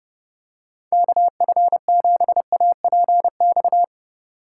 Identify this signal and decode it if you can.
Morse code, heard as a keyed tone, about 30 wpm, 700 Hz. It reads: KF7APX